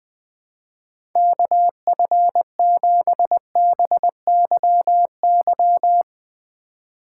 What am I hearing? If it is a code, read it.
Morse code, heard as a keyed tone, about 20 wpm, 700 Hz. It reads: KF7BYY
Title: Speed Racing - USA 2x3 Call Signs 20wpm